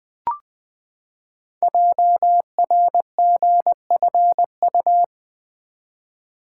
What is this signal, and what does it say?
Morse code, heard as a keyed tone, about 20 wpm, 700 Hz. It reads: JRGFU
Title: Code Groups - 5 Letters - 20wpm